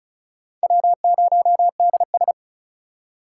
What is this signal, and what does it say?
Morse code, heard as a keyed tone, about 36 wpm, 700 Hz. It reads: W0DS
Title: USA Call Signs - 4 Characters - 3x Character Spacing - 35wpm